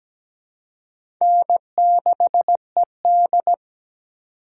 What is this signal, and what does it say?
Morse code, heard as a keyed tone, about 17 wpm, 700 Hz. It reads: N6ED
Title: USA Call Signs - 4 Characters - 4x Character Spacing - 17wpm